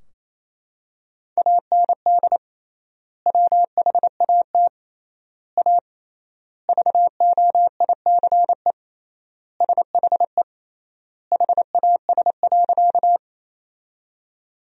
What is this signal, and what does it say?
Morse code, heard as a keyed tone, about 28 wpm, 700 Hz. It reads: AND WHAT A VOICE SHE HAS.
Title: 3x Word-Spacing - Sentences from Top 500 Words - 28wpm